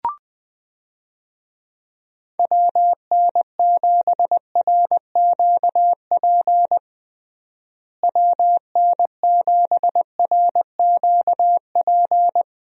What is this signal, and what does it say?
Morse code, heard as a keyed tone, about 21 wpm, 700 Hz. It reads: WN7RQP WN7RQP
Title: USA Call Signs - 6 Characters - Repeated 2x - 20wpm